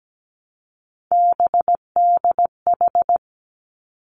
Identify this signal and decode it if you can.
Morse code, heard as a keyed tone, about 17 wpm, 700 Hz. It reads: BDH
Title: Three Challenging Letter-Number Combinations 17wpm